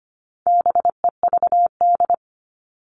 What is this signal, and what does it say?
Morse code, heard as a keyed tone, about 25 wpm, 700 Hz. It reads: BEVD
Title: All Cut Numbers - 4-Digit 25wpm